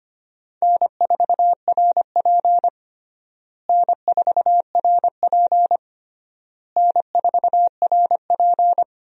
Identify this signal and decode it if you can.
Morse code, heard as a keyed tone, about 25 wpm, 700 Hz. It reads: N4RP N4RP N4RP
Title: USA Call Signs - 4 Characters - Repeated 3x - 25wpm